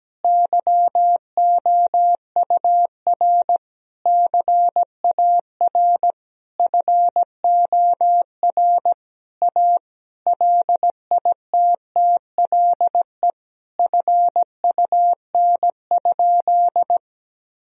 Morse code, heard as a keyed tone, about 17 wpm, 700 Hz. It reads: YOUR CAR FOR A LITTLE FUN?
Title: Sentences from Top 1000 Words - Review Part 1 - 17wpm